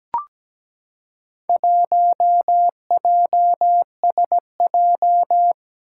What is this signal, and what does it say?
Morse code, heard as a keyed tone, about 17 wpm, 700 Hz. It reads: 1JSJ
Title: Four Challenging Letter-Number Combinations 17wpm